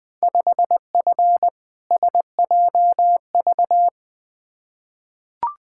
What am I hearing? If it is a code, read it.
Morse code, heard as a keyed tone, about 21 wpm, 700 Hz. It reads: 5F SJV
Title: Field Day Exchanges Encore - 20wpm